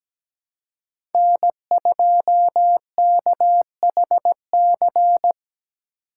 Morse code, heard as a keyed tone, about 17 wpm, 700 Hz. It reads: N2KHC
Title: USA Call Signs - 2x Character Spacing - 17wpm